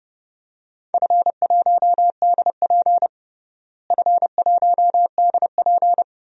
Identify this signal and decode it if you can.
Morse code, heard as a keyed tone, about 30 wpm, 700 Hz. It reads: F1DP F1DP
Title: DXCC-100 Call Signs - 4 Characters - Repeated 2x - 30wpm